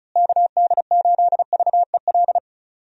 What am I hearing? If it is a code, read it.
Morse code, heard as a keyed tone, about 34 wpm, 700 Hz. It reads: KD8VEL
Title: USA Call Signs - 6 Characters - Repeated 3x - 35wpm